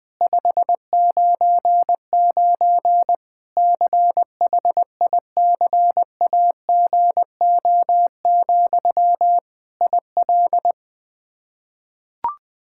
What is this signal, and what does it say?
Morse code, heard as a keyed tone, about 20 wpm, 700 Hz. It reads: 599 CHICAGO, IL
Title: Practice QSO - Exchange 1 - Beginning - 20wpm